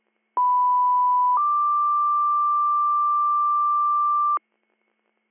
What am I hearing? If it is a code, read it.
Two-tone page, heard as a one-second tone, then a three-second tone, about 990 then 1180 Hz, with noise. It is heard from 0.4 to 4.4 s.